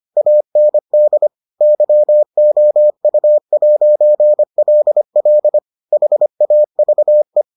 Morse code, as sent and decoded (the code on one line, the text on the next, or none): .- -. -.. / -.-- --- ..- .----. .-.. .-.. / .... .- ...- .
AND YOU'LL HAVE